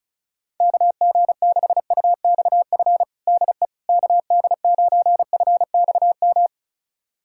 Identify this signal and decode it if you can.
Morse code, heard as a keyed tone, about 34 wpm, 700 Hz. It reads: KG6UXF DE KD9FXM
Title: USA Callsign DE Callsign 35wpm